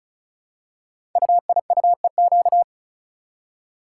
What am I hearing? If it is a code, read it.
Morse code, heard as a keyed tone, about 36 wpm, 690 Hz. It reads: UIUEQ
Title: Code Groups - 5 Letters - 35wpm